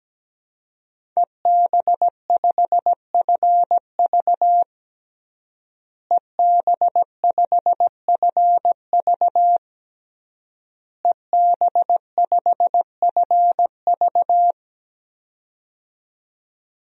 Morse code, heard as a keyed tone, about 17 wpm, 700 Hz. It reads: EB5FV EB5FV EB5FV
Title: DXCC-100 Call Signs - 5 Characters - Repeated 3x - 17wpm